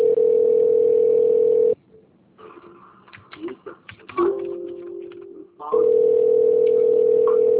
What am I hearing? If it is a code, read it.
Ringback tone (North America), 440 + 480 Hz.